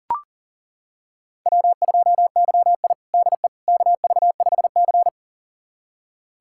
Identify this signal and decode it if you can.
Morse code, heard as a keyed tone, about 40 wpm, 700 Hz. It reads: W2YI DE KV5C
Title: USA Callsign DE Callsign - 4 Characters - 40wpm